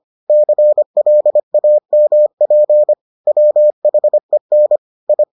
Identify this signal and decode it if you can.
Morse code, heard as a keyed tone, about 25 wpm, 600 Hz. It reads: CLAMP WHEN I